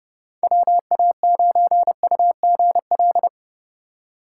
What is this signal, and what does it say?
Morse code, heard as a keyed tone, about 32 wpm, 700 Hz. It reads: WA9UGL
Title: USA Call Signs - 8x Character Spacing - 30wpm